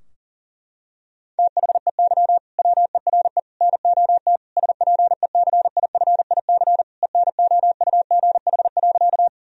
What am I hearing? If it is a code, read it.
Morse code, heard as a keyed tone, about 40 wpm, 700 Hz. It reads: THEY WERE NOT SPECIFIC ENOUGH.
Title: Sentences from Top 1000 Words - Review Part 3 - 40wpm